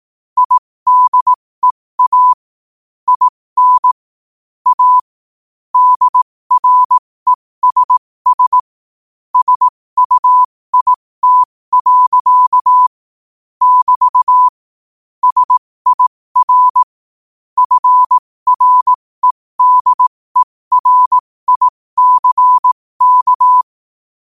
Morse code, heard as a keyed tone, about 18 wpm, spaced as for 15 wpm, 1000 Hz. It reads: IDEA IN A DRESS SUIT. <BT> SIR FREDERICK
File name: qotdfiles / 15wpm